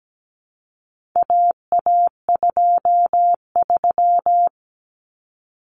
Morse code, heard as a keyed tone, about 17 wpm, 700 Hz. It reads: AA23
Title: Common Cut Numbers - 4-Digit 17wpm